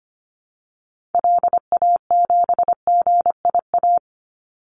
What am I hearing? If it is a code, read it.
Morse code, heard as a keyed tone, about 25 wpm, 700 Hz. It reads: LA7GIA
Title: Call Signs Repeated 2x - 25wpm